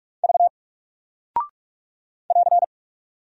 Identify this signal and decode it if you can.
Morse code, heard as a keyed tone, about 46 wpm, 700 Hz. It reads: V <AR>
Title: Single Character (Letter:Number:Punctuation:Prosign) Alt 45wpm